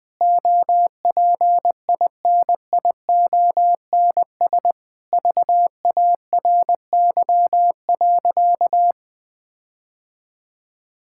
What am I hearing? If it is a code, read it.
Morse code, heard as a keyed tone, about 21 wpm, 700 Hz. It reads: OPINIONS VARY.
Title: Sentences with 7-Word Phrases - 20wpm